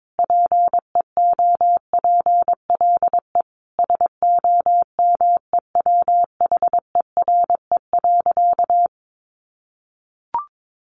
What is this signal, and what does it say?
Morse code, heard as a keyed tone, about 22 wpm, 700 Hz. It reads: PEOPLE SOMEWHERE.
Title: Sentences from Top 1400 Words 22wpm